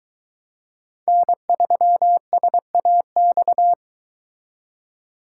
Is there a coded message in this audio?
Yes: Morse code from 1.1 to 3.7 s, heard as a keyed tone, about 23 wpm, 700 Hz. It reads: N3SAX